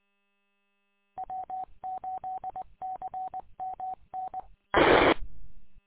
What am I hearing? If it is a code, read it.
Morse code, heard as a keyed tone, about 20 wpm, 750 Hz. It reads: W8CMN R
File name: w8cmn- -combined